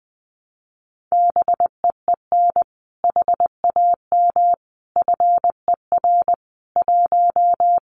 Morse code, heard as a keyed tone, about 20 wpm, 700 Hz. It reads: BEEN HAM FER 1
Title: Common QSO Phrases Encore 20wpm